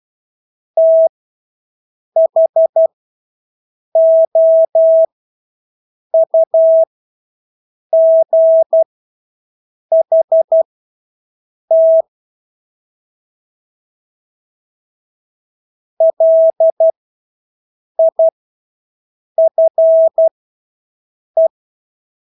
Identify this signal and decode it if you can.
Morse code, heard as a keyed tone, about 12 wpm, spaced as for 6 wpm, 650 Hz. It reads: THOUGHT LIFE
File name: Engelske ord 12-6 wpm